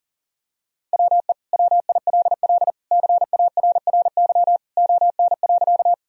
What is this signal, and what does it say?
Morse code, heard as a keyed tone, about 40 wpm, 700 Hz. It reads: WE WILL CARRY ON.
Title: Sentences from Top 1000 Words - Review Part 1 - 40wpm